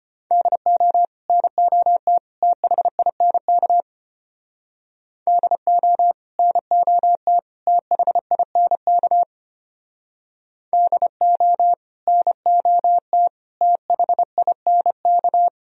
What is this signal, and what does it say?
Morse code, heard as a keyed tone, about 29 wpm, 700 Hz. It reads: DO NOT THINK DO NOT THINK DO NOT THINK
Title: Speed-Racing - Sets of 3 Words 25wpm